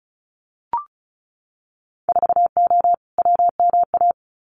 Morse code, heard as a keyed tone, about 36 wpm, 700 Hz. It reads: 4O WMA
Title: Winter Field Day Exchanges Encore 35wpm